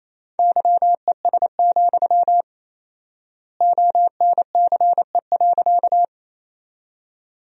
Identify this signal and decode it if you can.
Morse code, heard as a keyed tone, about 28 wpm, 700 Hz. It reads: YES, ONCE.
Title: 4x Word-Spacing - Sentences from Top 500 Words - 28wpm